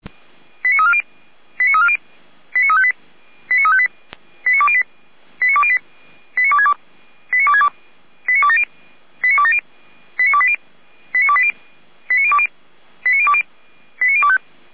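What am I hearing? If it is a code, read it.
Five-tone selective call (ZVEI), heard as quick runs of five tones, tones of about 70 ms. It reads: 89340 89340 89358 89358 89208 89262 89262 89270 89270 89280 89280 89020 89020 89025
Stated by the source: ZVEI five-tone selective call received with an SDR.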